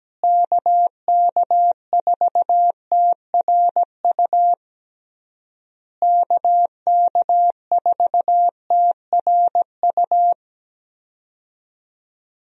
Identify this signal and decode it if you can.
Morse code, heard as a keyed tone, about 17 wpm, 700 Hz. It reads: KK4TRU KK4TRU
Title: USA Call Signs - 6 Characters - Repeated 3x - 17wpm